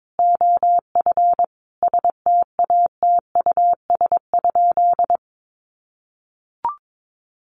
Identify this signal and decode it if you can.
Morse code, heard as a keyed tone, about 22 wpm, 700 Hz. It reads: OF STATUS?
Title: Sentences from Top 1400 Words 22wpm